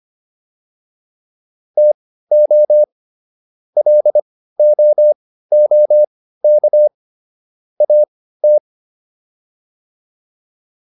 Morse code, heard as a keyed tone, about 25 wpm, spaced as for 15 wpm, 600 Hz. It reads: TO LOOK AT
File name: WD204 15